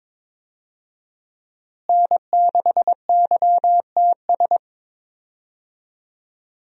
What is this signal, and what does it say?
Morse code, heard as a keyed tone, about 22 wpm, 700 Hz. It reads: N6YTS
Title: USA Call Signs - 5 Characters - Repeated 3x - 22wpm